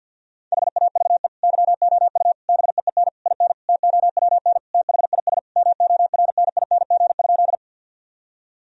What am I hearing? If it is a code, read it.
Morse code, heard as a keyed tone, about 52 wpm, 700 Hz. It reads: HAVE YOU BEEN IN TOWN THIS MORNING?
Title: Sentences from Top 1000 Words - Review Part 2 - 50wpm